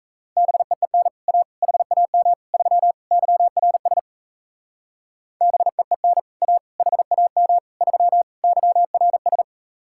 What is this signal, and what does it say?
Morse code, heard as a keyed tone, about 36 wpm, 700 Hz. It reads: BEEN A HAM 3 YRS BEEN A HAM 3 YRS